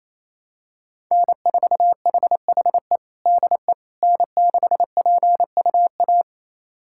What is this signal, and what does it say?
Morse code, heard as a keyed tone, about 27 wpm, 700 Hz. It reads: N4HHE DE N6PUA
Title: USA Callsign DE Callsign - 5 Characters - 28wpm